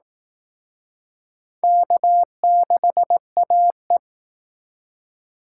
Morse code, heard as a keyed tone, about 19 wpm, 700 Hz. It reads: K6AE
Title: Speed Racing - USA 1x2 Call Signs 15wpm